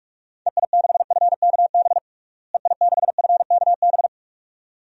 Morse code, heard as a keyed tone, about 46 wpm, 700 Hz. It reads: EI6FKB EI6FKB
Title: DXCC-100 Call Signs - 6 Characters - Repeated 2x - 45wpm